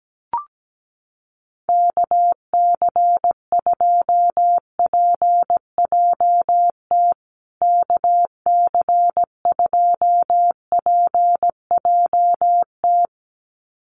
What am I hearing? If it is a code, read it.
Morse code, heard as a keyed tone, about 17 wpm, 700 Hz. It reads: KC2PJT KC2PJT
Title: USA Call Signs Repeated 2x 17wpm